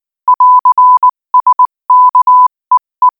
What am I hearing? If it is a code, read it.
Morse code, heard as a keyed tone, about 19 wpm, 1000 Hz. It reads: <AR> SK EE
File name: IZ4BEH-13-cm-AS